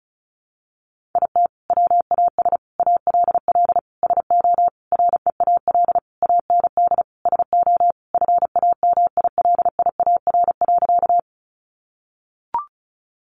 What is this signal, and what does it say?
Morse code, heard as a keyed tone, about 34 wpm, 700 Hz. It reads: IT WAS ALL SO REAL AND SO FAMILIAR.
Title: Sentences from Top 1000 Words 35wpm